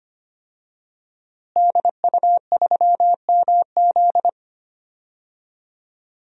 Morse code, heard as a keyed tone, about 25 wpm, 700 Hz. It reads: DU3MZ
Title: DXCC-100 Call Signs - 5 Characters - Repeated 2x - 25wpm